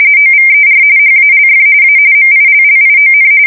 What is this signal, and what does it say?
RTTY, about 45.45 baud. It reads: OVER HALF A MILLION
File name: fsk170 / battleshipnj